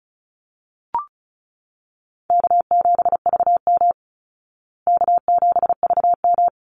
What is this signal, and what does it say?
Morse code, heard as a keyed tone, about 34 wpm, 700 Hz. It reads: K7VM K7VM
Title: Easy Call Signs Repeated 2x - 35wpm